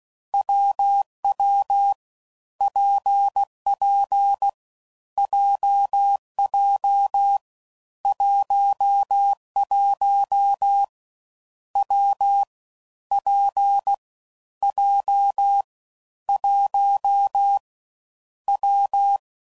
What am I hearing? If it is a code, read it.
Morse code, heard as a keyed tone, about 16 wpm, 800 Hz. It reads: WW PP JJ 11 W P J 1 W
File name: Combinations / Difficult Combos / WPJ WPJ1